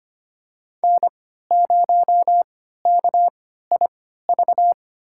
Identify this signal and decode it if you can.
Morse code, heard as a keyed tone, about 25 wpm, spaced as for 14 wpm, 700 Hz. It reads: N0KIV
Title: USA Call Signs - 5 Characters - 3x Character Spacing - 25wpm